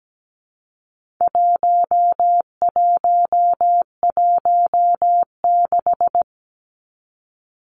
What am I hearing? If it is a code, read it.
Morse code, heard as a keyed tone, about 17 wpm, 700 Hz. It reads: 1116
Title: Numbers - 4-Digit 17wpm